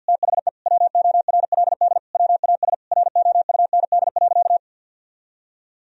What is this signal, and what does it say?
Morse code, heard as a keyed tone, about 50 wpm, 700 Hz. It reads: THE WORLD WAS ROUND.
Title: Sentences from Top 1000 Words - Review Part 1 - 50wpm